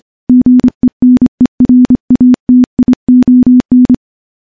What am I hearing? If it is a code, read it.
Morse code, heard as a keyed tone, about 25 wpm, 250 Hz. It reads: GENERATION
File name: morse-training-25WPM